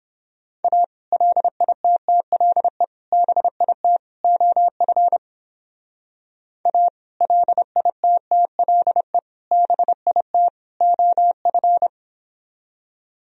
Morse code, heard as a keyed tone, about 26 wpm, 700 Hz. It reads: A LITTLE BIT OF A LITTLE BIT OF
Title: Speed-Racing - Sets of 4 Words 22wpm